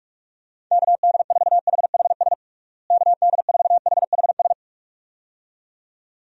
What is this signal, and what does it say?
Morse code, heard as a keyed tone, about 46 wpm, 700 Hz. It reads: KD4HHS KD4HHS
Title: USA Call Signs - 6 Characters - Repeated 2x - 45wpm